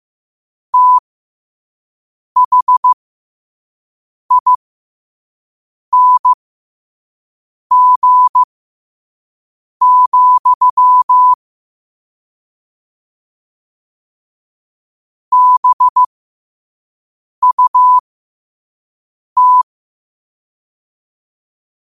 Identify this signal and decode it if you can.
Morse code, heard as a keyed tone, about 15 wpm, spaced as for 5 wpm, 1000 Hz. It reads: THING, BUT
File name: qotdfiles / 5wpm